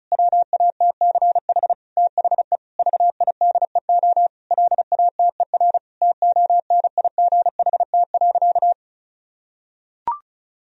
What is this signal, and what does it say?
Morse code, heard as a keyed tone, about 34 wpm, 700 Hz. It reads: WATCH THE VIDEO LATER TONIGHT.
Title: Sentences from Top 1000 Words - Review Part 4 - 35wpm